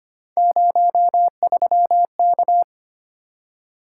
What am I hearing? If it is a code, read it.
Morse code, heard as a keyed tone, about 25 wpm, 700 Hz. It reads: 03K
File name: Three Letter-Number Combinations - 25wpm